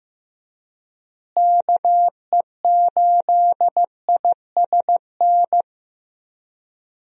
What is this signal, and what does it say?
Morse code, heard as a keyed tone, about 15 wpm, 700 Hz. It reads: KE8ISN
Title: USA Call Signs - 6 Characters - Repeated 2x - 15wpm